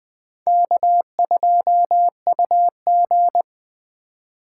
Morse code, heard as a keyed tone, about 19 wpm, 700 Hz. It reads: K2UG